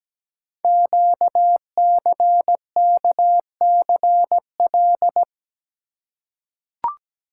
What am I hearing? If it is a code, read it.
Morse code, heard as a keyed tone, about 17 wpm, 700 Hz. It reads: QCKCL
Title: Code Groups - 5 Letters - 17wpm